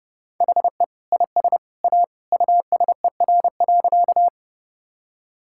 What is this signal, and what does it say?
Morse code, heard as a keyed tone, about 32 wpm, 700 Hz. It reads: HE IS A USER.
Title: Sentences from Top 1000 Words - Review Part 5 - 30wpm